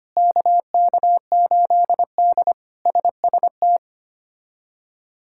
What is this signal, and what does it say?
Morse code, heard as a keyed tone, about 25 wpm, 700 Hz. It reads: KK8D SST
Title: K1USN SST QSO Practice - 25wpm